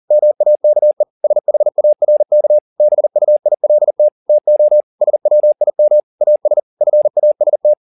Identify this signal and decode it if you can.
Morse code, heard as a keyed tone, about 40 wpm, 600 Hz. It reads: MAKE SHARK BUILT TO SWIM AS FAST
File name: the old man and the sea 2 40wpm